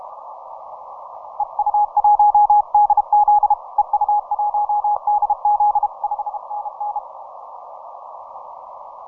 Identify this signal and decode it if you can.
Morse code, heard as a keyed tone, about 32 wpm, 830 Hz. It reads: EU1DZ EU1DZ 5NN
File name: simple-file-list / KH2 N2NL10MCW